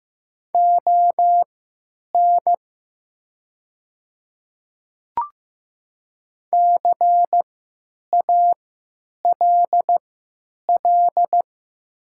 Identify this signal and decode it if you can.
Morse code, heard as a keyed tone, about 15 wpm, spaced as for 8 wpm, 700 Hz. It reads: ON CALL
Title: 3x Character Spacing - Top 100 Words - 15wpm